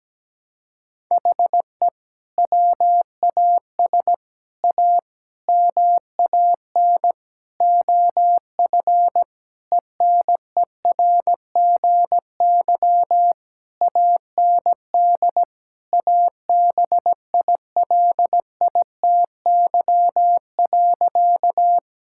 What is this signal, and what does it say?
Morse code, heard as a keyed tone, about 17 wpm, 700 Hz. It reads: HE WAS A MAN OF ENERGY AND ABILITY.
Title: Sentences from Top 1000 Words - Review Part 6 - 17wpm